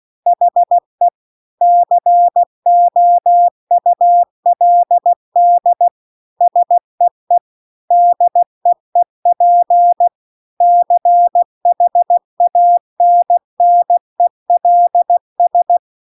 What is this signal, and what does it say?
Morse code, heard as a keyed tone, about 16 wpm, 700 Hz. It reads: HE COULD SEE DEEP CHANNELS